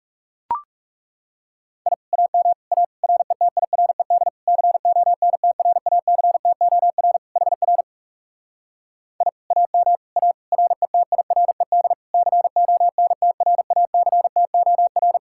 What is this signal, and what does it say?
Morse code, heard as a keyed tone, about 37 wpm, 700 Hz. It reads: I AM A RETIRED CONTRACTOR HR I AM A RETIRED CONTRACTOR
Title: Speed-Racing Common QSO Phrases - 30wpm